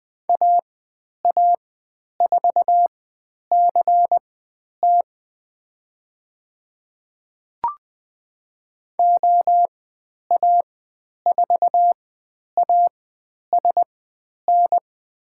Morse code, heard as a keyed tone, about 21 wpm, spaced as for 10 wpm, 700 Hz. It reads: AA4CT OA4ASN
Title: 102 Lesson 17g - Letter u - Call Signs 20wpm